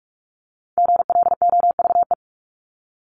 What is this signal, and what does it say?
Morse code, heard as a keyed tone, about 44 wpm, 700 Hz. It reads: GLOVE
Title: Top 1751-2000 Words 45wpm